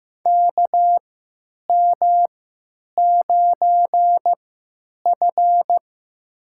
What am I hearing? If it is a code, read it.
Morse code, heard as a keyed tone, about 15 wpm, spaced as for 9 wpm, 700 Hz. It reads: KM9F
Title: USA Call Signs - 4 Characters - 3x Character Spacing - 15wpm